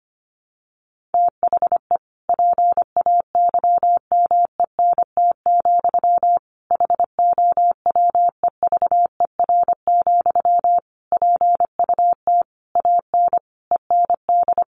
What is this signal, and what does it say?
Morse code, heard as a keyed tone, about 25 wpm, 700 Hz. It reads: THE PAYMENT, HOWEVER, PUT AN END
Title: Sentences from Top 1200 Words 25wpm